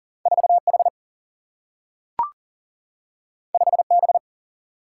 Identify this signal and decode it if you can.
Morse code, heard as a keyed tone, about 40 wpm, 700 Hz. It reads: 4 H 5 B
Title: Single-Letter Single-Number 40wpm